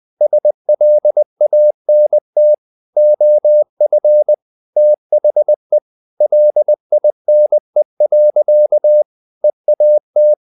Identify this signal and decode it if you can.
Morse code, heard as a keyed tone, about 20 wpm, 600 Hz. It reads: SLANT OF THE LINE. EAT
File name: the old man and the sea / the old man and the sea part 1 20wpm